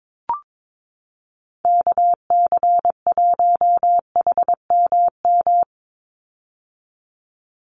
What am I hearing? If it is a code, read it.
Morse code, heard as a keyed tone, about 22 wpm, 700 Hz. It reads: KC1HMM
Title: USA Call Signs 22wpm